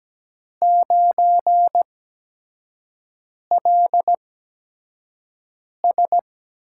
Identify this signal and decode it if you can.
Morse code, heard as a keyed tone, about 17 wpm, spaced as for 5 wpm, 700 Hz. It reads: 9LS
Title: USA Call Signs - 4 Characters - 8x Character Spacing - 17wpm